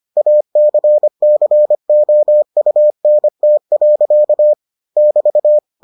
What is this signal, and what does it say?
Morse code, heard as a keyed tone, about 25 wpm, 600 Hz. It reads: ACCOUNT. <BT>